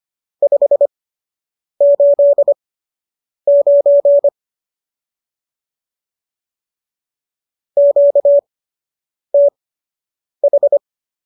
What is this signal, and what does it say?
Morse code, heard as a keyed tone, about 25 wpm, spaced as for 8 wpm, 570 Hz. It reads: 589 QTH
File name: qso003 08